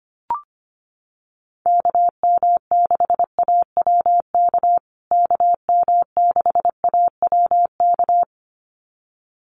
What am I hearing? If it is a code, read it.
Morse code, heard as a keyed tone, about 25 wpm, 700 Hz. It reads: KM6AWK KM6AWK
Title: USA Call Signs Repeated 2x 25wpm